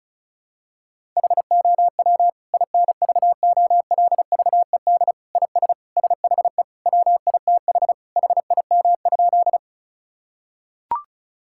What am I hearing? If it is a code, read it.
Morse code, heard as a keyed tone, about 36 wpm, 700 Hz. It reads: HOW INVOLVED IS SHE WITH HIM?
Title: Sentences from Top 1000 Words - Review Part 5 - 35wpm